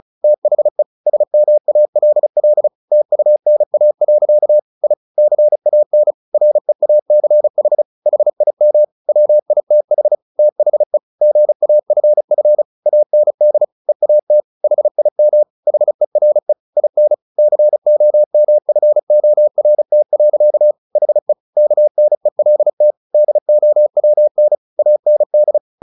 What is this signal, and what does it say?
Morse code, heard as a keyed tone, about 35 wpm, 600 Hz. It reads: THE SMALL TUNA. I CAN REACH HIM WITH THE GAFF AND EAT HIM HERE IN COMFORT. HE KNELT DOWN AND
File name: the old man and the sea / the old man and the sea 1 35wpm